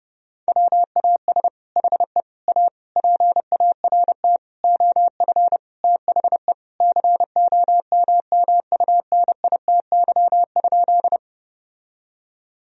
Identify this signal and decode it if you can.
Morse code, heard as a keyed tone, about 30 wpm, 700 Hz. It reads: WAS HE A PART OF THE COMMUNITY?
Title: Sentences from Top 1000 Words - Review Part 4 - 30wpm